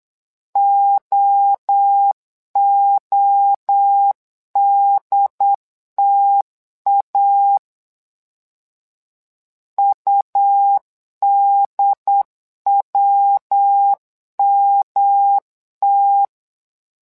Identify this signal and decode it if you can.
Morse code, heard as a keyed tone, about 8 wpm, 800 Hz. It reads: OODTA UDWMT
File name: LEZIONI / 3 MWRO / 3 MWRO 8wpm 2606